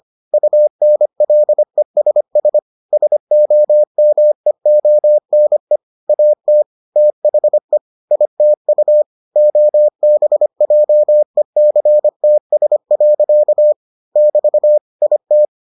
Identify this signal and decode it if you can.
Morse code, heard as a keyed tone, about 25 wpm, 600 Hz. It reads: UNLESS SOMEONE AT THE ITU OBJECTS. <BT> IT